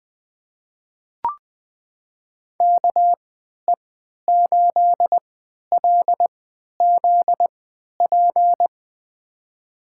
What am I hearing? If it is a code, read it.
Morse code, heard as a keyed tone, about 20 wpm, spaced as for 11 wpm, 700 Hz. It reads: KE8LZP
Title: USA Call Signs - 6 Characters - 3x Character Spacing - 20wpm